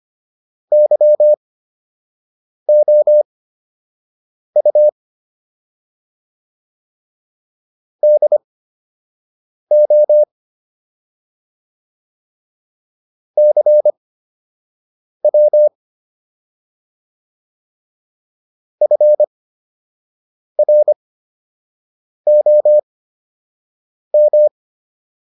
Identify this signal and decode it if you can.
Morse code, heard as a keyed tone, about 25 wpm, spaced as for 6 wpm, 600 Hz. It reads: YOU DO CW FROM